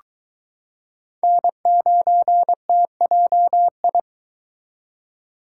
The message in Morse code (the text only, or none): N9TJI